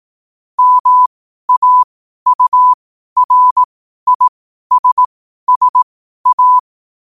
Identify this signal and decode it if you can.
Morse code, heard as a keyed tone, about 18 wpm, spaced as for 12 wpm, 1000 Hz. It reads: MAURISSA
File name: 2026-01-07-12wpm